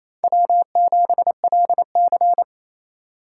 Morse code, heard as a keyed tone, about 36 wpm, 700 Hz. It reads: W7LC